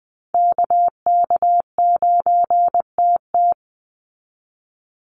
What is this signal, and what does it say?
Morse code, heard as a keyed tone, about 19 wpm, 700 Hz. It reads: KK9TT
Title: USA Call Signs 20wpm